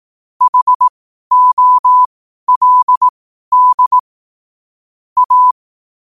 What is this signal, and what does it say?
Morse code, heard as a keyed tone, about 18 wpm, spaced as for 12 wpm, 1000 Hz. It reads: HOLD A
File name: qotdfiles / 12wpm